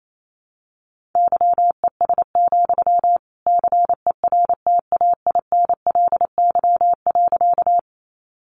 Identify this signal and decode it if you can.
Morse code, heard as a keyed tone, about 28 wpm, 700 Hz. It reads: YES, CERTAINLY.
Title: Sentences from Top 900 Words 28wpm